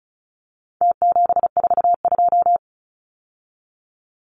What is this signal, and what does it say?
Morse code, heard as a keyed tone, about 34 wpm, 700 Hz. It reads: T742
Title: Numbers - Contest Style - 4-Digit 35wpm